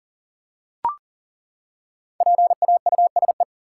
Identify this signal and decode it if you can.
Morse code, heard as a keyed tone, about 40 wpm, 700 Hz. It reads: PAUSE
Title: Five Letter Words - 40wpm